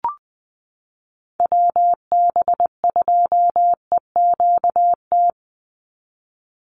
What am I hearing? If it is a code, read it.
Morse code, heard as a keyed tone, about 20 wpm, 700 Hz. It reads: WB2EQT
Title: USA Call Signs 20wpm